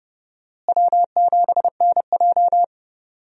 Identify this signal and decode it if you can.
Morse code, heard as a keyed tone, about 32 wpm, 700 Hz. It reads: W7NJ
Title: USA Call Signs - 4 Characters - Repeated 2x - 30wpm